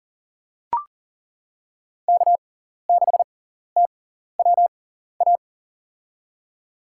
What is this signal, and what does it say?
Morse code, heard as a keyed tone, about 40 wpm, spaced as for 14 wpm, 700 Hz. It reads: K6TWA
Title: USA Call Signs - 5 Characters - 6x Character Spacing - 40wpm